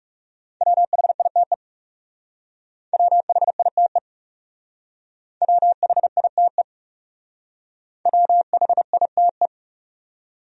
Morse code, heard as a keyed tone, about 36 wpm, 700 Hz. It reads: WHITE WHITE WHITE WHITE
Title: Speed-Racing Top 500 Words 30wpm